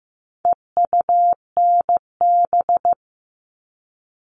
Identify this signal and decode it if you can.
Morse code, heard as a keyed tone, about 15 wpm, 700 Hz. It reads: EUNB
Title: All Cut Numbers - 4-Digit 15wpm